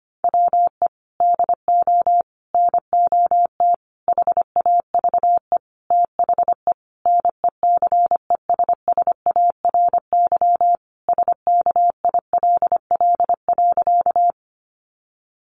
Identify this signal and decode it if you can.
Morse code, heard as a keyed tone, about 25 wpm, 700 Hz. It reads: WE DO NOT HAVE THE NECESSARY SKILL.
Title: Sentences from Top 1200 Words 25wpm